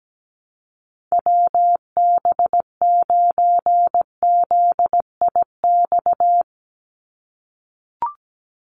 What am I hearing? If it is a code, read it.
Morse code, heard as a keyed tone, about 17 wpm, 700 Hz. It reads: WB9ZIX
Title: USA Call Signs 17wpm